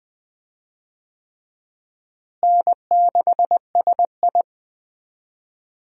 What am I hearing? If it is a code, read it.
Morse code, heard as a keyed tone, about 20 wpm, 700 Hz. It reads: N6SI
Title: USA Call Signs - 4 Characters - Repeated 3x - 20wpm